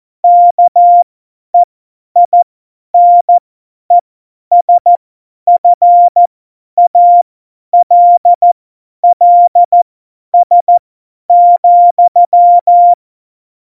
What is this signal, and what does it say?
Morse code, heard as a keyed tone, about 14 wpm, spaced as for 10 wpm, 700 Hz. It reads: KEINESFALLS,
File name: Morsekurs / Witze / Witz 07. Januar 2020, 50 F70